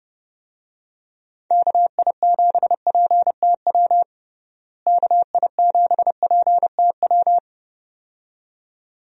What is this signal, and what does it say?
Morse code, heard as a keyed tone, about 30 wpm, 700 Hz. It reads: KI7PTW KI7PTW
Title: USA Call Signs - 6 Characters - Repeated 2x - 30wpm